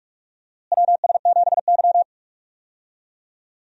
Morse code, heard as a keyed tone, about 42 wpm, 700 Hz. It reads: WS7Y
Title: USA Call Signs - 4 Characters - Repeated 2x - 45wpm